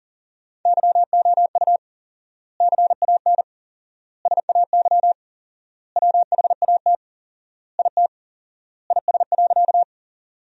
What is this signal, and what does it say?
Morse code, heard as a keyed tone, about 40 wpm, 700 Hz. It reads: YOU CAN SAY WHAT IT IS.
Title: 4x Word-Spacing - Sentences from Top 200 Words - 40wpm